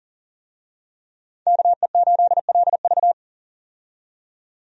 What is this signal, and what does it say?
Morse code, heard as a keyed tone, about 40 wpm, 700 Hz. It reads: KE8LV
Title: USA Call Signs - 5 Characters - 40wpm